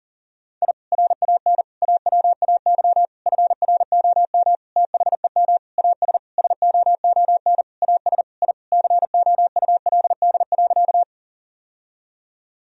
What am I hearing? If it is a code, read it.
Morse code, heard as a keyed tone, about 37 wpm, 700 Hz. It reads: I RAN AWAY FROM THEM AS SOON AS I COULD.
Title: Sentences from Top 1000 Words - Review Part 3 - 40wpm